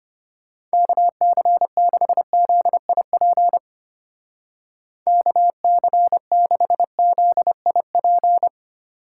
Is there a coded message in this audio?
Yes: Morse code from 0.7 to 8.5 s, heard as a keyed tone, about 32 wpm, 700 Hz. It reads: KC6ZIP KC6ZIP